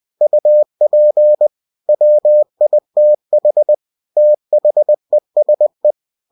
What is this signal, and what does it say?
Morse code, heard as a keyed tone, about 20 wpm, 600 Hz. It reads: UP WITH THESE